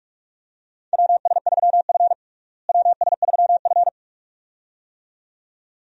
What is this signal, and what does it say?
Morse code, heard as a keyed tone, about 46 wpm, 700 Hz. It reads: WS3F WS3F
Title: USA Call Signs - 4 Characters - Repeated 2x - 45wpm